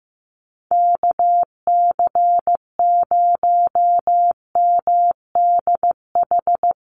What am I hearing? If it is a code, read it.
Morse code, heard as a keyed tone, about 15 wpm, 700 Hz. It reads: KC0MDH
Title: USA Call Signs 15wpm